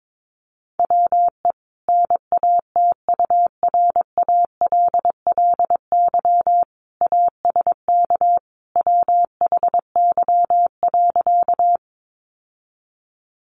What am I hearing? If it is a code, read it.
Morse code, heard as a keyed tone, about 22 wpm, 700 Hz. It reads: WE NATURALLY ASK WHY.
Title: Sentences from Top 1300 Words 22wpm